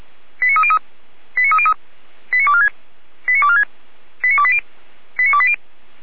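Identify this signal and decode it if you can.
Five-tone selective call (ZVEI), heard as quick runs of five tones, tones of about 70 ms. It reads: 89303 89303 89246 89246 89280 89280
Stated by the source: ZVEI five-tone selective call received by SDR, from the fire brigade at feuerwehr-orlamuende.org.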